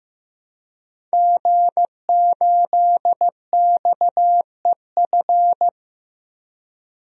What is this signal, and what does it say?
Morse code, heard as a keyed tone, about 16 wpm, 700 Hz. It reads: G8XEF
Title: DXCC-100 Call Signs - 5 Characters - Repeated 2x - 15wpm